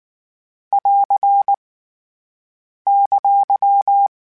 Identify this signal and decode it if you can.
Morse code, heard as a keyed tone, about 19 wpm, 800 Hz. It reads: <AR> !
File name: (spaced) / Q-Calls Spaced-mp3s / Q Calls sp Once @18